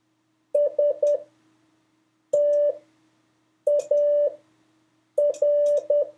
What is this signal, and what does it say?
Morse code, heard as a keyed tone, about 10 wpm, spaced as for 6 wpm, 580 Hz. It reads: STAR
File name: 4of5-words-3